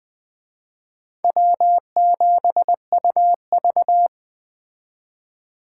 Morse code, heard as a keyed tone, about 20 wpm, 700 Hz. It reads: W7UV
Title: Speed Racing - USA 1x2 Call Signs 17wpm